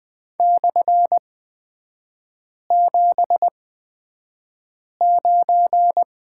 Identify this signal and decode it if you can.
Morse code, heard as a keyed tone, about 20 wpm, spaced as for 5 wpm, 690 Hz. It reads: /79
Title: 229 Lesson 37e - Letter x - Three Characters 20wpm